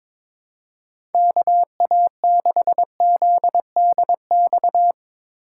Morse code, heard as a keyed tone, about 22 wpm, 700 Hz. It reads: KA6ZDX
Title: USA Call Signs - 6 Characters - 6x Character Spacing - 22wpm